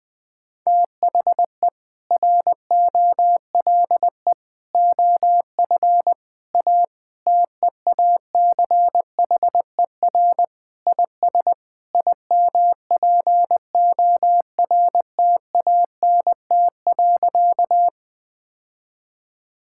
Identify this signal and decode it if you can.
Morse code, heard as a keyed tone, about 20 wpm, 700 Hz. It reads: THE ROLE OF A TEACHER IS IMPORTANT.
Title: Sentences with 3-Word Phrases - 20wpm